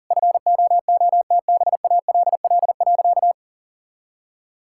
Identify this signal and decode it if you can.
Morse code, heard as a keyed tone, about 40 wpm, 700 Hz. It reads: FOOTBALL.
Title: Sentences from Top 1600 Words 40wpm